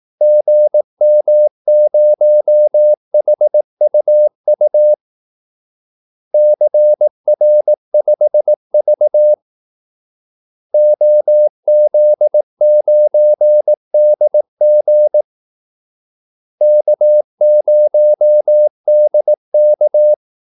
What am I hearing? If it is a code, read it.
Morse code, heard as a keyed tone, about 18 wpm, 600 Hz. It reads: GM0HUU CR5V OZ9DG K0DK
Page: Kaldesignaler 18 wpm